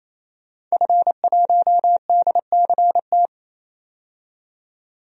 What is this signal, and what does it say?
Morse code, heard as a keyed tone, about 28 wpm, 700 Hz. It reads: F1DCT
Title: DXCC-100 Call Signs - 5 Characters - 28wpm